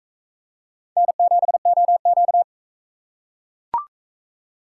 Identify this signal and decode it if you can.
Morse code, heard as a keyed tone, about 42 wpm, 700 Hz. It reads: N7OQ